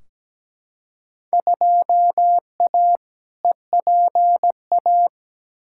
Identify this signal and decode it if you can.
Morse code, heard as a keyed tone, about 17 wpm, 700 Hz. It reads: 2A EPA
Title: Field Day Exchanges - 17wpm